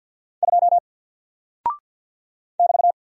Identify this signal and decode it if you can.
Morse code, heard as a keyed tone, about 49 wpm, 700 Hz. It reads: 2 <BT>
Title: Single Character (Letter:Number:Punctuation:Prosign) 50wpm